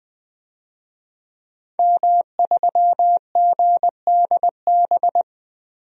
Morse code, heard as a keyed tone, about 20 wpm, 700 Hz. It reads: M3GDB
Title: DXCC-100 Call Signs - 5 Characters - 20wpm